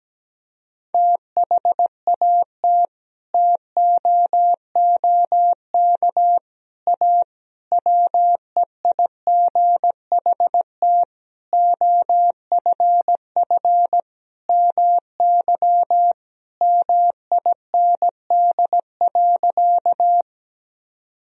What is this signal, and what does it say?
Morse code, heard as a keyed tone, about 17 wpm, 700 Hz. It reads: THAT TOOK A WEIGHT OFF MY MIND.
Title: Sentences from Top 1000 Words - Review Part 3 - 17wpm